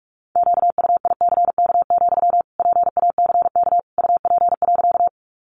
Morse code, heard as a keyed tone, about 44 wpm, 700 Hz. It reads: QUICK, PACK UP.
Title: Sentences from Top 1300 Words 45wpm